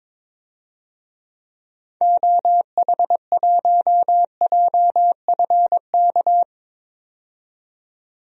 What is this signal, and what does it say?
Morse code, heard as a keyed tone, about 22 wpm, 700 Hz. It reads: OH1JFK
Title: DXCC-100 Call Signs - 6 Characters - 22wpm